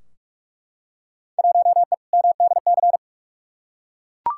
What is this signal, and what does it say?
Morse code, heard as a keyed tone, about 41 wpm, 700 Hz. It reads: 1E MDC
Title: Field Day Exchanges - 45wpm